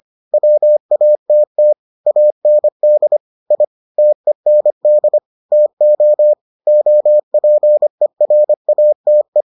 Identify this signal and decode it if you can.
Morse code, heard as a keyed tone, about 25 wpm, 600 Hz. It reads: WATT AND I TEND TO OPERATE